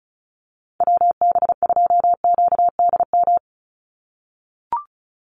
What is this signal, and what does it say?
Morse code, heard as a keyed tone, about 34 wpm, 700 Hz. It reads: WB2QDM
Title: USA Call Signs 35wpm